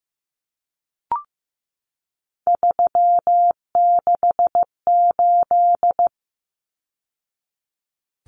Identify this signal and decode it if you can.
Morse code, heard as a keyed tone, about 15 wpm, 700 Hz. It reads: 368
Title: Numbers - 3-Digit 15wpm